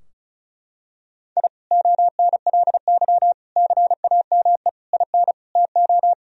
Morse code, heard as a keyed tone, about 36 wpm, 700 Hz. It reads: I ONLY CAME IN TO
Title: Sentences from Top 1000 Words - Review Part 2 - 35wpm